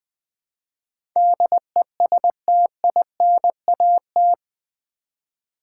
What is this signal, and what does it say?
Morse code, heard as a keyed tone, about 20 wpm, 700 Hz. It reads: DESTINAT
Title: Build A Word - 4 Letters At A Time - 11-Letter Words - 20wpm